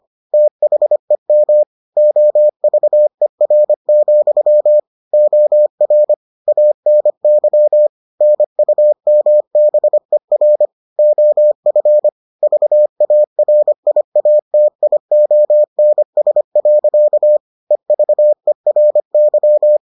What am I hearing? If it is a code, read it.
Morse code, heard as a keyed tone, about 25 wpm, 600 Hz. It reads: THEM OVER, OR ANY NUMBER OF VARIATIONS. EVERY